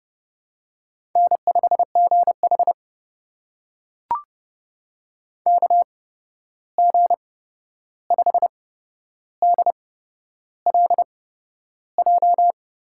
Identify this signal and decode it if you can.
Morse code, heard as a keyed tone, about 30 wpm, spaced as for 11 wpm, 700 Hz. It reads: N5GH KG5DLJ
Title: USA Call Signs - 8x Character Spacing - 30wpm